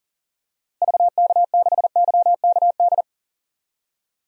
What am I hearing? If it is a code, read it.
Morse code, heard as a keyed tone, about 37 wpm, 700 Hz. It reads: VK6YKD
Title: DXCC-100 Call Signs - 6 Characters - Repeated 2x - 40wpm